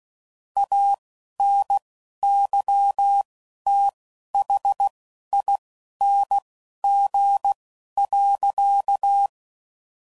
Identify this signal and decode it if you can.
Morse code, heard as a keyed tone, about 16 wpm, spaced as for 11 wpm, 800 Hz. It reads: ANYTHING.